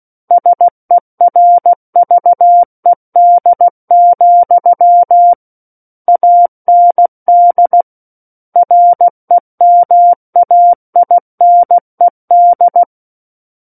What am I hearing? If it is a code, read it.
Morse code, heard as a keyed tone, about 16 wpm, 700 Hz. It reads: SERVED, AND REMAINED
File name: Book One / 07-Chapter 7 - 16 WPM